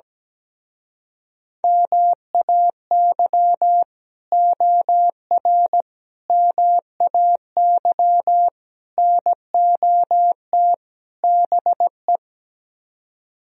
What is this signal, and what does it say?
Morse code, heard as a keyed tone, about 17 wpm, 700 Hz. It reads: MAY OR MAY NOT BE
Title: Speed-Racing - Sets of 5 Words 17wpm